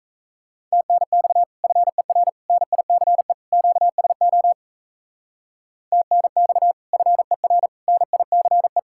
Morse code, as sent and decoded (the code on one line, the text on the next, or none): - -. -..- / ..-. .-. / -. .. -.-. / --.- ... --- / - -. -..- / ..-. .-. / -. .. -.-. .
TNX FER NICE QSO TNX FER NICE